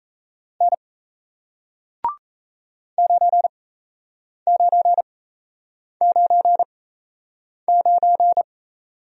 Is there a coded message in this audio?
Yes: Morse code from 0.6 to 8.4 s, heard as a keyed tone, about 38 wpm, spaced as for 8 wpm, 700 Hz. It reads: N 9999